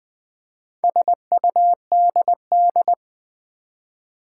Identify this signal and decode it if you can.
Morse code, heard as a keyed tone, about 20 wpm, 700 Hz. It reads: SUDD
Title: Build A Word - 2 Letters At A Time - 8-Letter Words - 20wpm